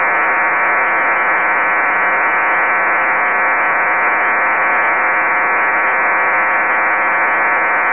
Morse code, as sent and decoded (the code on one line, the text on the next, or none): none